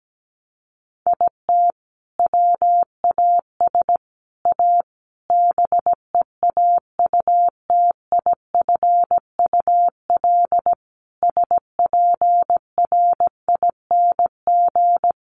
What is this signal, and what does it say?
Morse code, heard as a keyed tone, about 17 wpm, 700 Hz. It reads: IT WAS A BEAUTIFUL SPRING
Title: Sentences from Top 1200 Words 17wpm